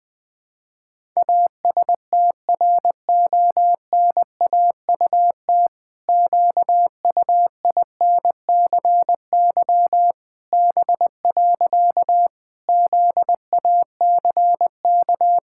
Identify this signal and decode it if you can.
Morse code, heard as a keyed tone, about 20 wpm, 700 Hz. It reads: ASTRONAUT QUINCY B. ZACK
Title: Pangram Challenge v2 - 20wpm